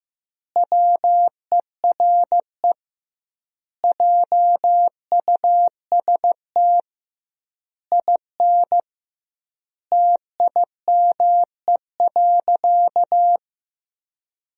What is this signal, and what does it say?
Morse code, heard as a keyed tone, about 16 wpm, 700 Hz. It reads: WERE JUST IN TIME.
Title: 2x Word-Spacing - Sentences from Top 300 Words - 15wpm